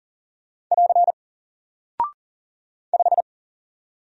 Morse code, heard as a keyed tone, about 40 wpm, 700 Hz. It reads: <AR> 5
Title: Single Character (Letter:Number:Punctuation:Prosign) Alt 40wpm